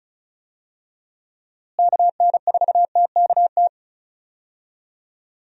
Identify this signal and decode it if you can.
Morse code, heard as a keyed tone, about 34 wpm, 700 Hz. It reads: KN4TKT
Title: USA Call Signs - 6 Characters - Repeated 3x - 35wpm